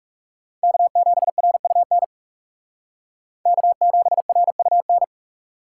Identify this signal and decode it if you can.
Morse code, heard as a keyed tone, about 36 wpm, 700 Hz. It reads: K7RUN K7RUN
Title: Speed Racing - USA 1x3 Call Signs 30wpm